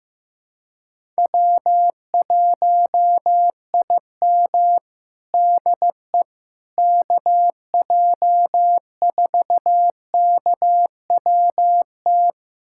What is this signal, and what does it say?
Morse code, heard as a keyed tone, about 16 wpm, 700 Hz. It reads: W1IM DE KJ4KWT
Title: USA Callsign DE Callsign 15wpm